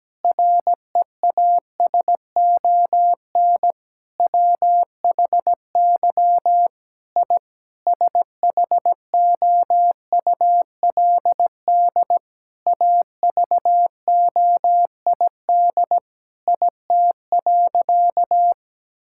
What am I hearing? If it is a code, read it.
Morse code, heard as a keyed tone, about 17 wpm, 700 Hz. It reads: REASON WHY I SHOULD AVOID IT.
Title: Sentences from Top 1000 Words - Review Part 6 - 17wpm